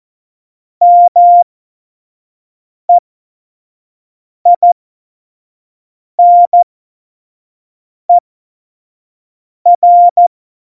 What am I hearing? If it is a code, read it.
Morse code, heard as a keyed tone, about 14 wpm, spaced as for 5 wpm, 700 Hz. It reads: MEINER
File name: Morsekurs / Witze / Witz 17.Dezember 2019, 25 F70